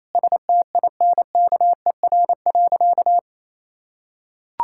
Morse code, heard as a keyed tone, about 28 wpm, 700 Hz. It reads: STINKER.
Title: Pangram Challenge v2 - 28wpm